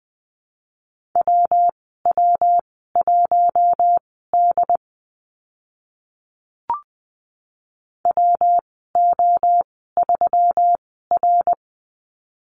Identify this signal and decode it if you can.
Morse code, heard as a keyed tone, about 20 wpm, spaced as for 15 wpm, 700 Hz. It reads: WW1D WO3R
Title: 167 Lesson 27g - Number 7 - Call Signs 20wpm